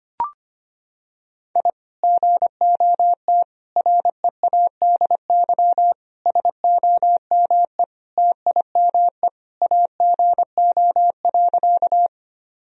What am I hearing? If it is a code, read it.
Morse code, heard as a keyed tone, about 25 wpm, 700 Hz. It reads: I GOT READY SOME TIME AGO.
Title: Sentences from Top 1000 Words - Review Part 3 - 25wpm